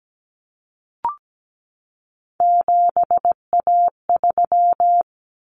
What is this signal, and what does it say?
Morse code, heard as a keyed tone, about 17 wpm, 700 Hz. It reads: 7A3
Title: Common Cut Numbers - 3-Digit 17wpm